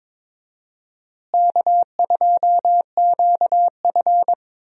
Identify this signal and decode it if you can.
Morse code, heard as a keyed tone, about 22 wpm, 700 Hz. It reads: K2QF
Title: USA Call Signs - 4 Characters - Repeated 2x - 22wpm